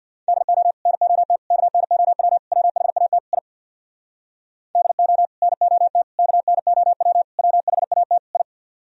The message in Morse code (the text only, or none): DO NOT KNOW WHAT I DO NOT KNOW WHAT I